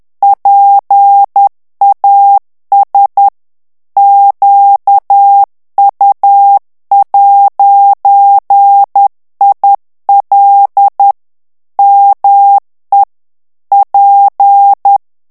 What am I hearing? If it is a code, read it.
Morse code, heard as a keyed tone, about 10 wpm, 800 Hz. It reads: PAS QU'IL ME P
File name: FRANCAIS TEXTE CLAIR 2 VITESSE 10 MOTS MINUTES